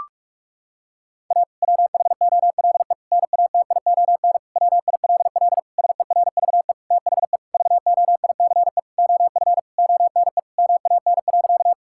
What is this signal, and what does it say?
Morse code, heard as a keyed tone, about 46 wpm, 700 Hz. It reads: A WHOLE NATION WILL SERVE THE VOICE OF ONE MAN.
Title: Sentences from Top 1000 Words - Review Part 6 - 45wpm